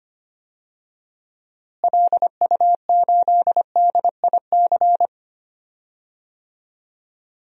Morse code, heard as a keyed tone, about 25 wpm, 700 Hz. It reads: LU8DIC
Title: DXCC-100 Call Signs - 6 Characters - Repeated 2x - 25wpm